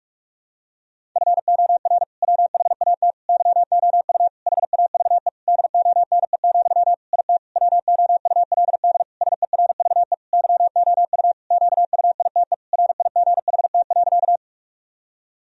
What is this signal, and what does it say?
Morse code, heard as a keyed tone, about 46 wpm, 700 Hz. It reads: FOR WHAT YOU HAVE DONE, IT WOULD SERVE YOU QUITE RIGHT.
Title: Sentences from Top 1000 Words - Review Part 2 - 45wpm